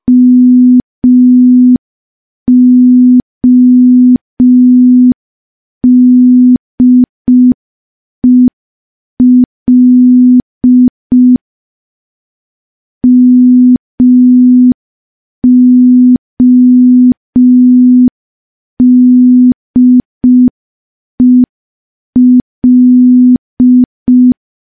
Morse code, heard as a keyed tone, about 5 wpm, 250 Hz. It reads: MODEL MODEL
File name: morse / morse-training-05WPM